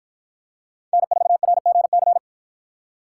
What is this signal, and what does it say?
Morse code, heard as a keyed tone, about 53 wpm, 700 Hz. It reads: N4RGC